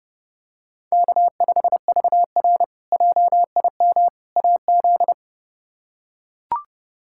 Morse code, heard as a keyed tone, about 30 wpm, 700 Hz. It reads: K5VR JIM AZ
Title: K1USN SST QSO Practice - 30wpm